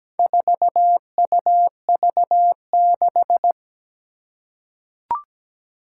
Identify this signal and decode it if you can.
Morse code, heard as a keyed tone, about 17 wpm, 700 Hz. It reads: 4UV6
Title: Four Challenging Letter-Number Combinations 17wpm